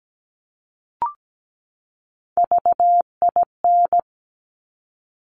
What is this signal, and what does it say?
Morse code, heard as a keyed tone, about 17 wpm, 700 Hz. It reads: VIN
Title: Top Three Letter Combinations in Words 17wpm